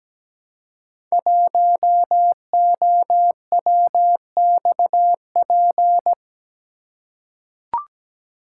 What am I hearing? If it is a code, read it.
Morse code, heard as a keyed tone, about 17 wpm, 700 Hz. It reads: 1OWXP
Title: Code Groups - 5 Letter:Numbers - 17wpm